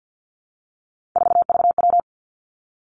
Morse code, heard as a keyed tone, about 50 wpm, 700 Hz. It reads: 4VW
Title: Three Challenging Letter-Number Combinations 50wpm